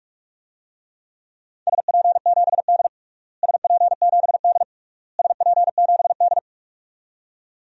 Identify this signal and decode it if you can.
Morse code, heard as a keyed tone, about 46 wpm, 700 Hz. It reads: SP7D SP7D SP7D
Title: DXCC-100 Call Signs - 4 Characters - Repeated 3x - 45wpm